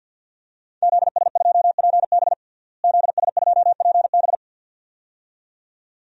Morse code, heard as a keyed tone, about 52 wpm, 700 Hz. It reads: ZS2PB ZS2PB
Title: DXCC-100 Call Signs - 5 Characters - Repeated 2x - 50wpm